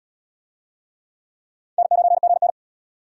Morse code, heard as a keyed tone, about 77 wpm, 700 Hz. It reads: N9GN